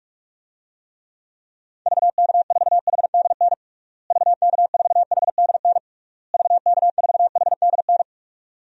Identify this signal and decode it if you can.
Morse code, heard as a keyed tone, about 41 wpm, 700 Hz. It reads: VK4HDN VK4HDN VK4HDN
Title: DXCC-100 Call Signs - 6 Characters - Repeated 3x - 45wpm